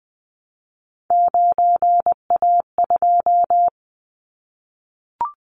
Morse code, heard as a keyed tone, about 19 wpm, 700 Hz. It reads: 9A2
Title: Callsign Prefixes for Top 100 DXCC - 20wpm